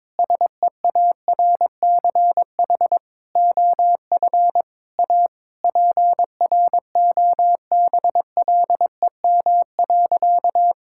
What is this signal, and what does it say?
Morse code, heard as a keyed tone, about 22 wpm, 700 Hz. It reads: SEARCH OF A PROBLEM.
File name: Sentences with 7-Word Phrases - 22wpm